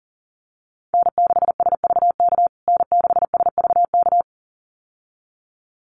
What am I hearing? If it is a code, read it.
Morse code, heard as a keyed tone, about 40 wpm, 700 Hz. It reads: N6SVK N6SVK
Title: USA Call Signs Repeated 2x 40wpm